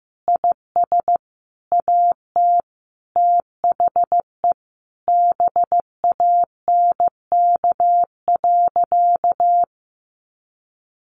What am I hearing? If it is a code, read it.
Morse code, heard as a keyed tone, about 15 wpm, 700 Hz. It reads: IS AT THE BANK.
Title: Sentences from Top 1300 Words 15wpm